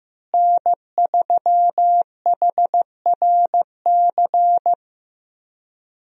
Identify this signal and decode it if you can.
Morse code, heard as a keyed tone, about 15 wpm, 700 Hz. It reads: N3HRC
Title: USA Call Signs - 5 Characters - Repeated 3x - 15wpm